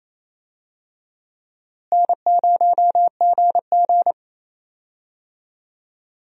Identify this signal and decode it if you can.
Morse code, heard as a keyed tone, about 28 wpm, 700 Hz. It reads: N0GG
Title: USA Call Signs - 4 Characters - Repeated 3x - 28wpm